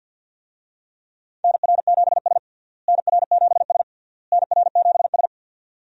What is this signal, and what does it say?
Morse code, heard as a keyed tone, about 51 wpm, 700 Hz. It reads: NR7S NR7S NR7S
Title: USA Call Signs - 4 Characters - Repeated 3x - 50wpm